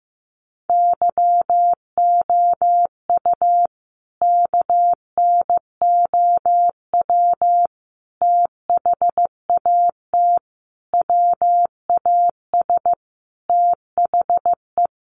Morse code, heard as a keyed tone, about 14 wpm, 700 Hz. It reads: YOU KNOW THAT WAS THE
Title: Sentences from Top 1500 Words 15wpm